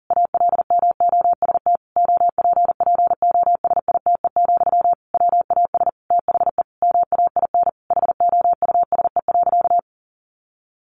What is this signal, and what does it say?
Morse code, heard as a keyed tone, about 40 wpm, 700 Hz. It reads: ALMOST OPPOSITE, WAS THE MAIN HOUSE.
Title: Sentences from Top 1300 Words 40wpm